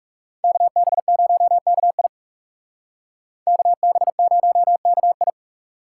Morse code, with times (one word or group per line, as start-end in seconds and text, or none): 0.4-2.1 KB0KI
3.5-5.3 KB0KI